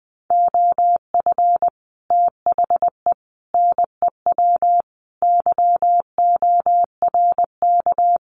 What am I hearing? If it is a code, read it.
Morse code, heard as a keyed tone, about 20 wpm, 700 Hz. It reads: OF THE NEW YORK
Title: Sets of 5 Words 20wpm